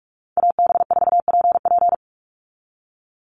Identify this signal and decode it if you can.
Morse code, heard as a keyed tone, about 44 wpm, 700 Hz. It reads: AB4PP
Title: NA SOTA Chaser Callsigns 45wpm